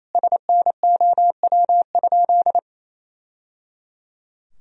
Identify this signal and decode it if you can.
Morse code, heard as a keyed tone, about 28 wpm, 700 Hz. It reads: SNOW?
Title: Sentences from Top 1000 Words - Review Part 3 - 28wpm